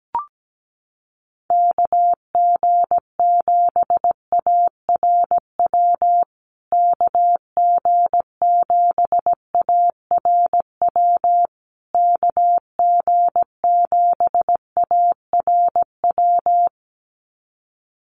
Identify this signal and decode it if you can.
Morse code, heard as a keyed tone, about 17 wpm, 700 Hz. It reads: KG7ARW KG7ARW KG7ARW
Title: USA Call Signs Repeated 3x 17wpm